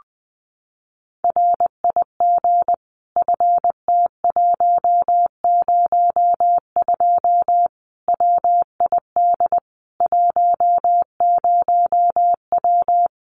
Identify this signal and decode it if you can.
Morse code, heard as a keyed tone, about 20 wpm, 700 Hz. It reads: RIG FT102 WID 10W
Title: Common QSO Phrases 20wpm